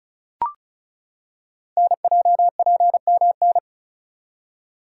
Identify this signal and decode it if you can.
Morse code, heard as a keyed tone, about 36 wpm, 700 Hz. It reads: NJPMN
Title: Code Groups - 5 Letters - 35wpm